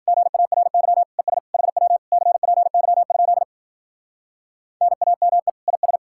Morse code, heard as a keyed tone, about 52 wpm, 700 Hz. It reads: GARY ES HW CPY? NAME IS